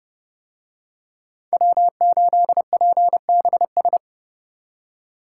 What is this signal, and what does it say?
Morse code, heard as a keyed tone, about 28 wpm, 700 Hz. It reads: W8PBS